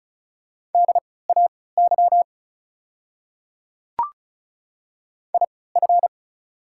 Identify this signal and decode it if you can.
Morse code, heard as a keyed tone, about 34 wpm, spaced as for 20 wpm, 700 Hz. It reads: DAY IF
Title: 3x Character Spacing - Top 300 Words - 35wpm